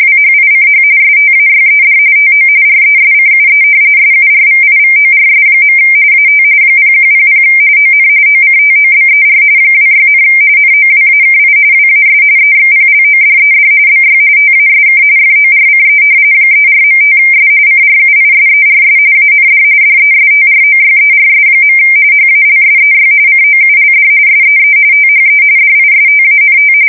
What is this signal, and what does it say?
RTTY, about 45.45 baud. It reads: WHICH STAYED TO RESCUE SURVIVORS. AT 0100 ON OCTOBER 26, NOWAKI WAS EXPLODED BY TASK FORCE 34.5S CRUISERS AND DESTROYERS. HALSEY COULD SEE THE DISTANT
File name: fsk170 / battleshipnj